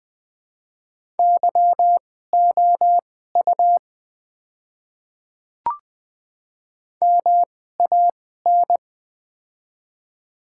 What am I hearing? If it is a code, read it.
Morse code, heard as a keyed tone, about 20 wpm, spaced as for 14 wpm, 700 Hz. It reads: YOU MAN
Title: 2x Character Spacing - Top 200 Words - 20wpm